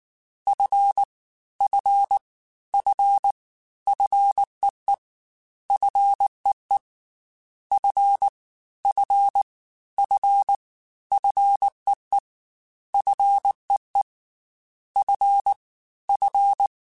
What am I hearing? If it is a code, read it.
Morse code, heard as a keyed tone, about 19 wpm, 800 Hz. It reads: F F F FEE FEE F F F FEE FEE F F
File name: Fs / Single F mp3s / Single F Prefixes 3x2 Twice @18